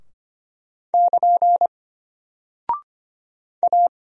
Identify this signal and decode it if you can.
Morse code, heard as a keyed tone, about 25 wpm, 700 Hz. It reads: <KN> A
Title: Single Character (Letter:Number:Punctuation:Prosign) Alt 25wpm